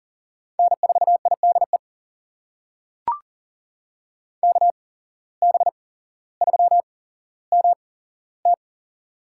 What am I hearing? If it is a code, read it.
Morse code, heard as a keyed tone, about 40 wpm, spaced as for 17 wpm, 700 Hz. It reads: N4IDE KB3MT
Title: USA Call Signs - 5 Characters - 8x Character Spacing - 40wpm